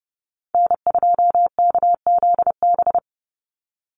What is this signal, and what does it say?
Morse code, heard as a keyed tone, about 30 wpm, 700 Hz. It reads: N2KZB
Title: USA Call Signs 30wpm